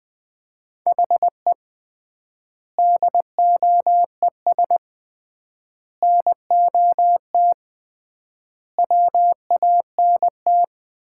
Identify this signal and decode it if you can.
Morse code, heard as a keyed tone, about 20 wpm, 700 Hz. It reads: HE DOES NOT WANT
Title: 3x Word-Spacing - Sets of 5 Words - 20wpm